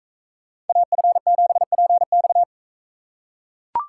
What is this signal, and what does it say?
Morse code, heard as a keyed tone, about 43 wpm, 700 Hz. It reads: AF7PX